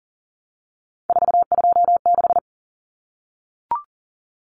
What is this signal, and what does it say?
Morse code, heard as a keyed tone, about 40 wpm, 700 Hz. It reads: 426
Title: Common Cut Numbers - 3-Digit 40wpm